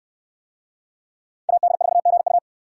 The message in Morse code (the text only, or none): AD4GU